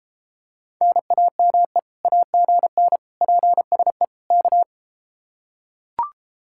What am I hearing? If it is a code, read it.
Morse code, heard as a keyed tone, about 33 wpm, 700 Hz. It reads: NAME AGN PSE K